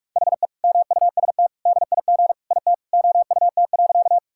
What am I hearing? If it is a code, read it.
Morse code, heard as a keyed tone, about 41 wpm, 700 Hz. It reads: HE MUST DIG IT OUT.
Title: Sentences from Top 1600 Words 45wpm